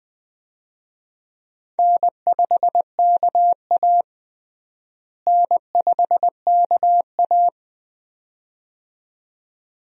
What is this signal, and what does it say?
Morse code, heard as a keyed tone, about 20 wpm, 700 Hz. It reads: N5KA N5KA
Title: USA Call Signs - 4 Characters - Repeated 2x - 20wpm